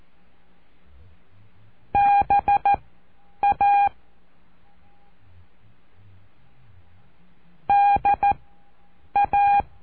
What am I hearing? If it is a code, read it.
Morse code, heard as a keyed tone, about 14 wpm, spaced as for 8 wpm, 810 Hz. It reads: BA DA
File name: cursos / CURSO DE TELEGRAFIA / Telegrafia - Trilha 05